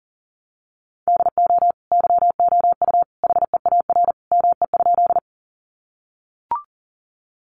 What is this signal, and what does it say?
Morse code, heard as a keyed tone, about 40 wpm, 700 Hz. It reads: DO YOU HEAR ME?
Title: Sentences from Top 300 Words 40wpm